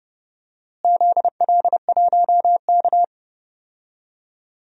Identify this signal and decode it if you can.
Morse code, heard as a keyed tone, about 30 wpm, 700 Hz. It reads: ZL1K
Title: DXCC-100 Call Signs - 4 Characters - Repeated 2x - 30wpm